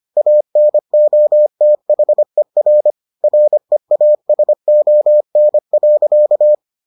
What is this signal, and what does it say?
Morse code, heard as a keyed tone, about 25 wpm, 600 Hz. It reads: ANOTHER REASON.